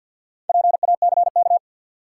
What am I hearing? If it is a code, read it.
Morse code, heard as a keyed tone, about 53 wpm, 700 Hz. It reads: PACK
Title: Sentences from Top 1300 Words - New Word Drill Alt - 50wpm